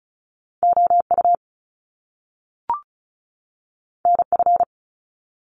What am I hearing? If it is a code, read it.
Morse code, heard as a keyed tone, about 36 wpm, 700 Hz. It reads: OU NF
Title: Top Two Letter Combinations in Words 35wpm